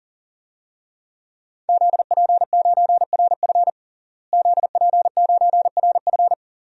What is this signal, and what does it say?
Morse code, heard as a keyed tone, about 37 wpm, 700 Hz. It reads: ZP9RF ZP9RF
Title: DXCC-100 Call Signs - 5 Characters - Repeated 3x - 40wpm